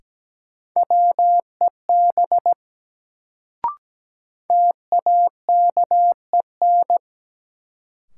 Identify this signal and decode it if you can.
Morse code, heard as a keyed tone, about 17 wpm, 700 Hz. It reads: WEB TAKEN
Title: Sentences from Top 1000 Words - Review Part 4 - New Word Drill - 17wpm